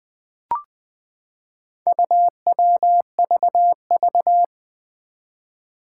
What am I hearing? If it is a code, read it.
Morse code, heard as a keyed tone, about 21 wpm, 700 Hz. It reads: UWVV
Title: Four Challenging Letter-Number Combinations 20wpm